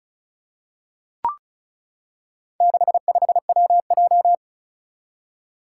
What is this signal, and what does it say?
Morse code, heard as a keyed tone, about 36 wpm, 700 Hz. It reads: 65WJ
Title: Four Challenging Letter-Number Combinations 35wpm